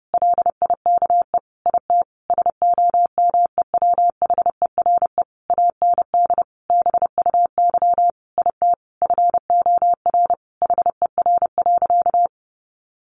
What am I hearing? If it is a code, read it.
Morse code, heard as a keyed tone, about 30 wpm, 700 Hz. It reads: LIKE IT SOMEWHERE AND BUY IT FOR HER.
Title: Sentences from Top 1400 Words 30wpm